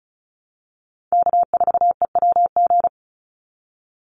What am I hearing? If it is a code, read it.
Morse code, heard as a keyed tone, about 34 wpm, 700 Hz. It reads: K4EWG
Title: Call Signs Repeated 2x - 35wpm